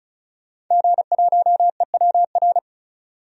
Morse code, heard as a keyed tone, about 36 wpm, 700 Hz. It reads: G1EWR
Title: DXCC-100 Call Signs - 5 Characters - 35wpm